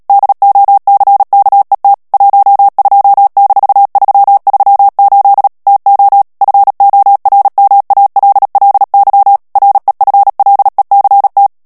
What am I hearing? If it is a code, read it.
Morse code, heard as a keyed tone, about 37 wpm, 800 Hz. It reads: DOCKET 12-338 TO FORMALLY REFLECT
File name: morse mp3 / ANGLAIS texte clair 3 vitesse 35 mots minute